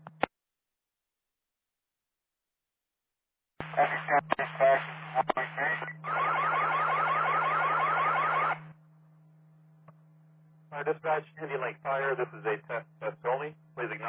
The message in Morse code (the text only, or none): none